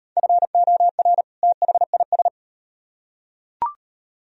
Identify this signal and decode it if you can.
Morse code, heard as a keyed tone, about 38 wpm, 700 Hz. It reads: FOR THIS